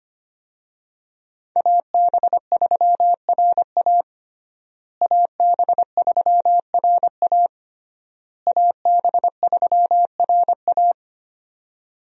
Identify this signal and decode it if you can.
Morse code, heard as a keyed tone, about 25 wpm, 700 Hz. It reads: AB3RA AB3RA AB3RA
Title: USA Call Signs - 5 Characters - Repeated 3x - 25wpm